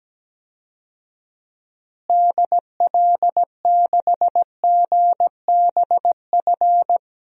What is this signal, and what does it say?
Morse code, heard as a keyed tone, about 17 wpm, 700 Hz. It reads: DL6GBF
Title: DXCC-100 Call Signs - 6 Characters - Repeated 2x - 17wpm